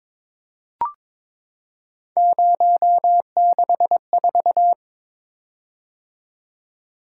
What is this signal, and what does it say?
Morse code, heard as a keyed tone, about 21 wpm, 700 Hz. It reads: 064
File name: practice / 22wpm / Three Letter-Number Combinations - 22wpm